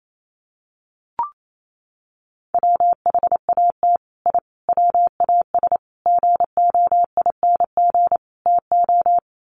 Morse code, heard as a keyed tone, about 28 wpm, 700 Hz. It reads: WHAT I WAS GOING TO
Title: Sets of 5 Words Encore2 28wpm